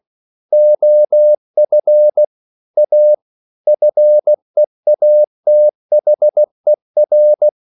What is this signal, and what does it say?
Morse code, heard as a keyed tone, about 16 wpm, 600 Hz. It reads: OF A FEATHER
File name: CW / Practice / EnglishSayings16